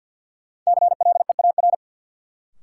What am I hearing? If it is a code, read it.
Morse code, heard as a keyed tone, about 51 wpm, 700 Hz. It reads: CLEAR
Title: Sets of 5 Words Encore - New Word Drill - 50wpm